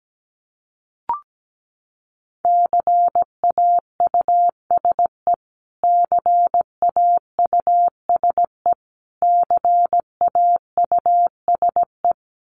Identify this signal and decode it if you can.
Morse code, heard as a keyed tone, about 17 wpm, 700 Hz. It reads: CAUSE CAUSE CAUSE
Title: Top 300 Words Repeated 17wpm